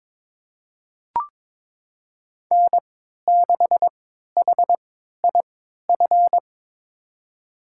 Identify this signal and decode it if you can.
Morse code, heard as a keyed tone, about 22 wpm, spaced as for 13 wpm, 700 Hz. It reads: N6HIF
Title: USA Call Signs - 5 Characters - 3x Character Spacing - 22wpm